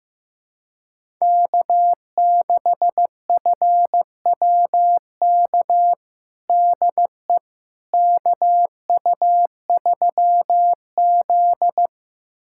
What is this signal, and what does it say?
Morse code, heard as a keyed tone, about 15 wpm, 700 Hz. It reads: K6FWK DE KU3Z
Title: USA Callsign DE Callsign 15wpm